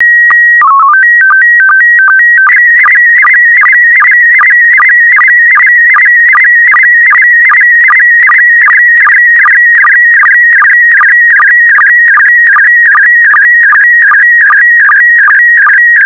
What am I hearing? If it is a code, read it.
SSTV (slow-scan television), PD 50.
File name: SSTV PD50 Leo